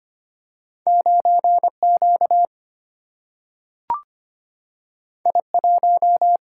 Morse code, heard as a keyed tone, about 25 wpm, 700 Hz. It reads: 9Q I1
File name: Two Letter-Number Combinations - 25wpm